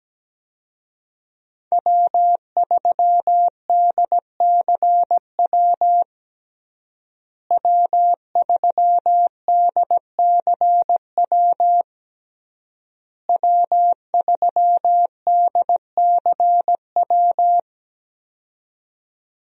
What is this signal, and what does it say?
Morse code, heard as a keyed tone, about 17 wpm, 700 Hz. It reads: W3DCW W3DCW W3DCW
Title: USA Call Signs - 5 Characters - Repeated 3x - 17wpm